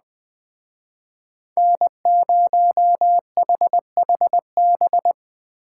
Morse code, heard as a keyed tone, about 20 wpm, 700 Hz. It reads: N0HHB